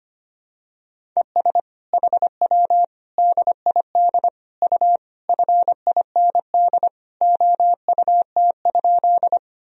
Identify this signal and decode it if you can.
Morse code, heard as a keyed tone, about 25 wpm, 700 Hz. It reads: ES HW DID U FIND OUT?
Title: Sentences from Top 200 Words and Abbreviations 25wpm